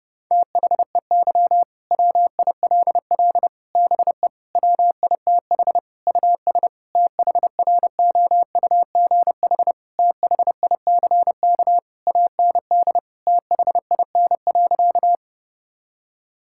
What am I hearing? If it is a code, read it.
Morse code, heard as a keyed tone, about 32 wpm, 700 Hz. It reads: THEY WILL BE WITH US THROUGH THICK AND THIN.
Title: Sentences from Top 1600 Words 30wpm